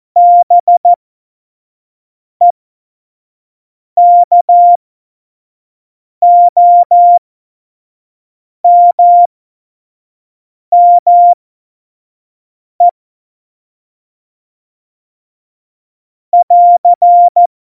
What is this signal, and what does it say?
Morse code, heard as a keyed tone, about 14 wpm, spaced as for 5 wpm, 700 Hz. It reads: BEKOMME <AR>
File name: Morsekurs / Witze / Witz 27. Februar 2018, 25 F70